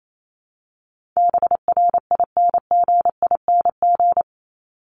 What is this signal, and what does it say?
Morse code, heard as a keyed tone, about 28 wpm, 700 Hz. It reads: BRINGING
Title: ING Suffix - Top 100 Words with Base Words 28wpm